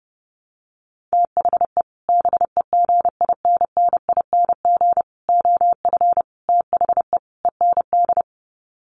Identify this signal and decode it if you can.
Morse code, heard as a keyed tone, about 30 wpm, 700 Hz. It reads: THE BEGINNING OF THE END
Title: Sets of 5 Words Encore2 30wpm